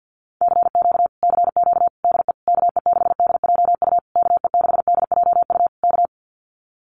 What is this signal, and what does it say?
Morse code, heard as a keyed tone, about 50 wpm, 700 Hz. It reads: CQ CQ DE KE6DJU KE6DJU K
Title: Common QSO Phrases Encore 50wpm